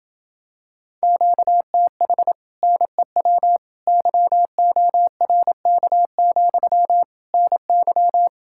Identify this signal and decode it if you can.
Morse code, heard as a keyed tone, about 27 wpm, 700 Hz. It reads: QTH NEW YORK, NY